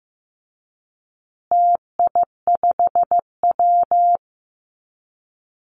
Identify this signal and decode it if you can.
Morse code, heard as a keyed tone, about 15 wpm, 700 Hz. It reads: TI5W
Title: Call Signs Repeated 2x - 15wpm